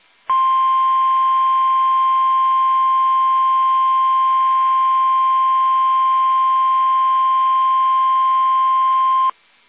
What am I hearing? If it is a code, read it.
Emergency Alert System (SAME) tones (the 1050 Hz attention tone). It is heard from 0.2 to 9.3 s.
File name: eas / KUSO / KUSO